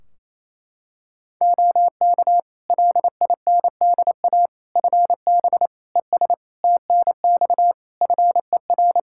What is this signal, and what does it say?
Morse code, heard as a keyed tone, about 27 wpm, 700 Hz. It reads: OK LINDA FB ES TNX FER
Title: Practice QSO - Exchange 2 - Info - 28wpm